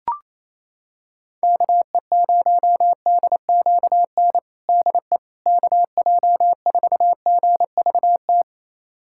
Morse code, heard as a keyed tone, about 28 wpm, 700 Hz. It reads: KE0DQN DE KJ4GVT
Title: USA Callsign DE Callsign - 6 Characters - 28wpm